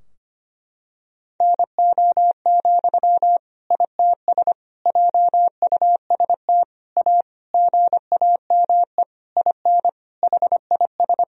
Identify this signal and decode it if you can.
Morse code, heard as a keyed tone, about 25 wpm, 700 Hz. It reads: NO, ITS JUST A GAME IN HIS
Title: Sentences from Top 1000 Words - Review Part 3 - 25wpm